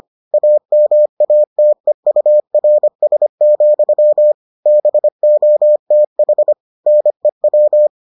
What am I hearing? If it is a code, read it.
Morse code, heard as a keyed tone, about 25 wpm, 600 Hz. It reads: AMATEURS, BOTH NEW